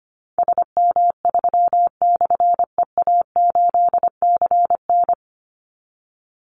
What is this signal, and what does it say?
Morse code, heard as a keyed tone, about 25 wpm, 700 Hz. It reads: SM3/EA8CN
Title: Call Signs Repeated 2x - 25wpm